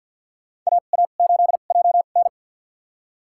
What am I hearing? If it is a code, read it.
Morse code, heard as a keyed tone, about 50 wpm, spaced as for 36 wpm, 700 Hz. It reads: AA8JN
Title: USA Call Signs - 2x Character Spacing - 50wpm